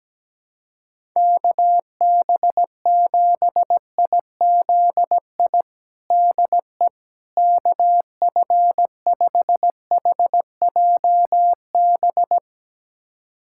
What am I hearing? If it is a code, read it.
Morse code, heard as a keyed tone, about 17 wpm, 700 Hz. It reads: KB7IZI DE KF5HJB
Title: USA Callsign DE Callsign - 6 Characters - 17wpm